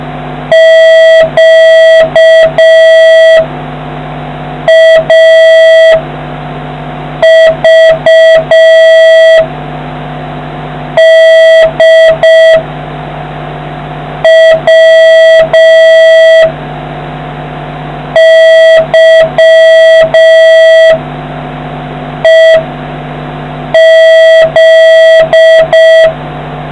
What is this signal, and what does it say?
Morse code, heard as a keyed tone, about 5 wpm, 1950 Hz, with noise. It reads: Q A V D W Y E Z